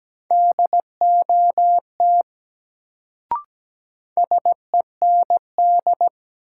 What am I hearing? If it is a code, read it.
Morse code, heard as a keyed tone, about 17 wpm, 700 Hz. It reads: DOT SEND
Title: Sentences from Top 1500 Words - New Word Drill Alt - 17wpm